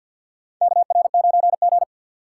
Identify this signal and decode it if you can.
Morse code, heard as a keyed tone, about 49 wpm, 700 Hz. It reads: KR9G
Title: USA Call Signs - 4 Characters - 6x Character Spacing - 50wpm